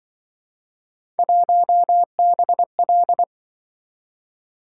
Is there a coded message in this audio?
Yes: Morse code from 1.2 to 3.2 s, heard as a keyed tone, about 28 wpm, 700 Hz. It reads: 1BL